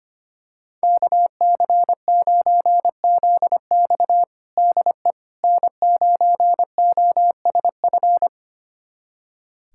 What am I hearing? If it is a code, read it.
Morse code, heard as a keyed tone, about 25 wpm, 700 Hz. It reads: KC9ZX DE N9OSF
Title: USA Callsign DE Callsign - 5 Characters - 25wpm